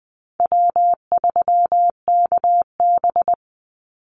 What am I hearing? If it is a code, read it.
Morse code, heard as a keyed tone, about 20 wpm, 700 Hz. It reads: W3KB
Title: Easy Call Signs Repeated 2x - 20wpm